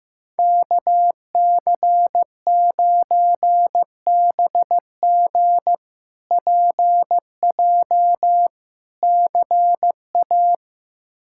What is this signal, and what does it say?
Morse code, heard as a keyed tone, about 16 wpm, 700 Hz. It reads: KC9BG PJ CA
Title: K1USN SST QSO Practice - 15wpm